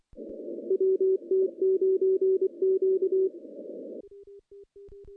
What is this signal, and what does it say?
Morse code, heard as a keyed tone, about 23 wpm, 380 Hz. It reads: WT9Q
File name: contests / CW-SS-k5tr- -0400